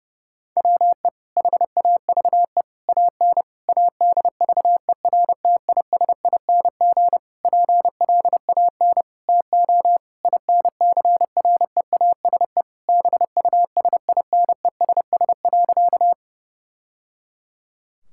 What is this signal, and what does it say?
Morse code, heard as a keyed tone, about 30 wpm, 700 Hz. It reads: WE HAVE AN ADVERTISING PLAN TO INCREASE BUSINESS.
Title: Sentences from Top 1000 Words - Review Part 6 - 30wpm